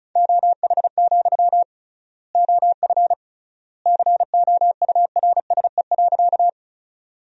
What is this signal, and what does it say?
Morse code, heard as a keyed tone, about 34 wpm, 700 Hz. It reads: OH, OF COURSE.
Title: 3x Word-Spacing - Sentences from Top 500 Words - 35wpm